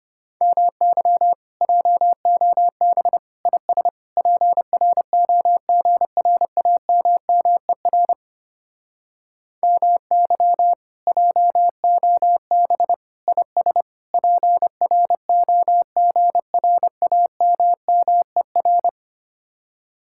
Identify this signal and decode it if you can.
Morse code, heard as a keyed tone, about 28 wpm, 700 Hz. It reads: MY JOB IS PROGRAMMER MY JOB IS PROGRAMMER
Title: Speed-Racing Common QSO Phrases - 25wpm